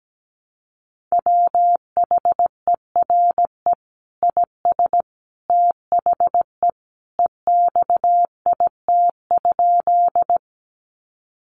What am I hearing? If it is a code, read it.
Morse code, heard as a keyed tone, about 17 wpm, 700 Hz. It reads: WHERE IS THE EXIT?
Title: Sentences from Top 1100 Words 17wpm